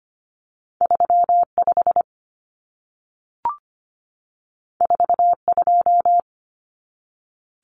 Morse code, heard as a keyed tone, about 25 wpm, 700 Hz. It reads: 35 42
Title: Common Cut Numbers - 2-Digit 25wpm